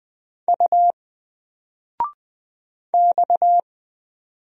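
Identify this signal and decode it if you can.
Morse code, heard as a keyed tone, about 20 wpm, 700 Hz. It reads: U X
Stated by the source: Single letter and number Morse drill at 20 wpm, from N4LCD.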